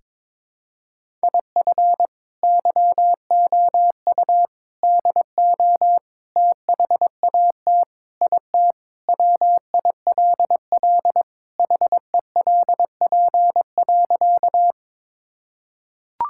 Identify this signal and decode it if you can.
Morse code, heard as a keyed tone, about 22 wpm, 700 Hz. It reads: IF YOU DO THAT IT WILL HELP.
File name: practice / 22wpm / Sentences with 3-Word Phrases - 22wpm